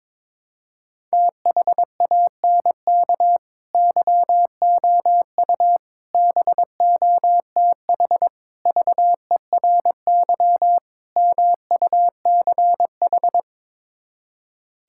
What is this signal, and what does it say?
Morse code, heard as a keyed tone, about 21 wpm, 700 Hz. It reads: THANK YOU BOTH VERY MUCH
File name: practice / 22wpm / Build A Phrase - 5-Word Phrases - 22wpm